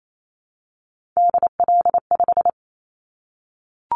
Morse code, heard as a keyed tone, about 28 wpm, 700 Hz. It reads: DL5
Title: Callsign Prefixes for Top 50 DXCC - 28wpm